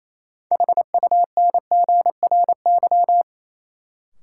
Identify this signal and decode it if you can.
Morse code, heard as a keyed tone, about 28 wpm, 700 Hz. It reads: HUNGRY
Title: Sentences from Top 1400 Words - New Word Drill Alt - 28wpm